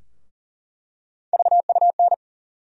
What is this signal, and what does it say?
Morse code, heard as a keyed tone, about 40 wpm, 700 Hz. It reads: VUN
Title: USA Call Sign Postfixes - 3 Letters 40wpm